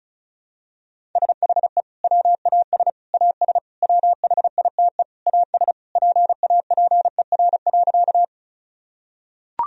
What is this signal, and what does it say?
Morse code, heard as a keyed tone, about 36 wpm, 700 Hz. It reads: SHE WAS AS WHITE AS PAPER.
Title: Sentences from Top 1000 Words - Review Part 1 - 35wpm